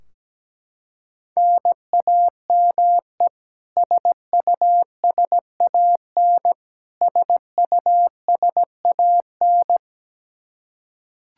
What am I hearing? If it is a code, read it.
Morse code, heard as a keyed tone, about 17 wpm, 700 Hz. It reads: NAME SUSAN SUSAN
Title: Practice QSO - Exchange 1 - Beginning - 17wpm